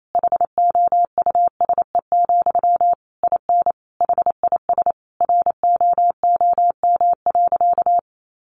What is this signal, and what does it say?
Morse code, heard as a keyed tone, about 28 wpm, 700 Hz. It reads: HOUSE, IN HIS ROOM.
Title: Sentences from Top 300 Words 28wpm